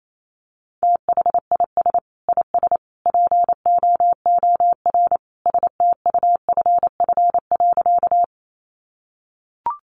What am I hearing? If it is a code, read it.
Morse code, heard as a keyed tone, about 28 wpm, 700 Hz. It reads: THIS IS POOR STUFF.
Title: Sentences from Top 1400 Words 28wpm